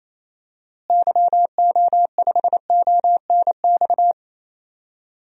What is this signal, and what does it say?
Morse code, heard as a keyed tone, about 28 wpm, 700 Hz. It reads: YO5ONX
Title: DXCC-100 Call Signs - 6 Characters - Repeated 2x - 28wpm